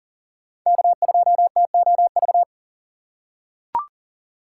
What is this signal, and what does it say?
Morse code, heard as a keyed tone, about 40 wpm, 700 Hz. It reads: K2TOV
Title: USA Call Signs - 5 Characters - 3x Character Spacing - 40wpm